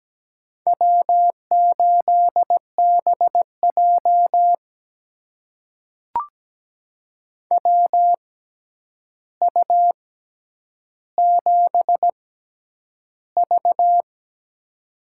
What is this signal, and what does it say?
Morse code, heard as a keyed tone, about 17 wpm, spaced as for 9 wpm, 700 Hz. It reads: W8BJ WU7V
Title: USA Call Signs - 6x Character Spacing - 17wpm